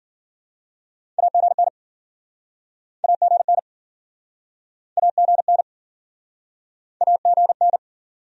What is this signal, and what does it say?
Morse code, heard as a keyed tone, about 51 wpm, 700 Hz. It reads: AGN AGN AGN AGN